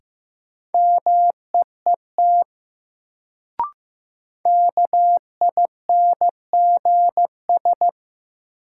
Morse code, heard as a keyed tone, about 15 wpm, 700 Hz. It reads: MEET KINGS
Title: Sentences from Top 1000 Words - Review Part 6 - New Word Drill - 15wpm